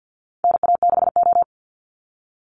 Morse code, heard as a keyed tone, about 50 wpm, 700 Hz. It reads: NA6O
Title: NA SOTA Chaser Callsigns 50wpm